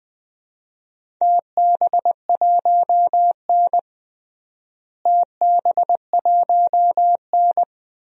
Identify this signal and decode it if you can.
Morse code, heard as a keyed tone, about 20 wpm, 700 Hz. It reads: TB1N TB1N
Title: DXCC-100 Call Signs - 4 Characters - Repeated 2x - 20wpm